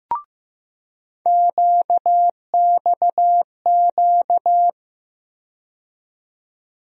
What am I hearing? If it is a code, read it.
Morse code, heard as a keyed tone, about 15 wpm, 700 Hz. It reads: QXQ
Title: USA Call Sign Postfixes - 3 Letters 15wpm